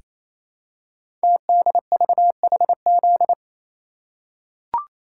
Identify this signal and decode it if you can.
Morse code, heard as a keyed tone, about 28 wpm, 690 Hz. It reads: TDVHZ
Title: Code Groups - 5 Letters - 28wpm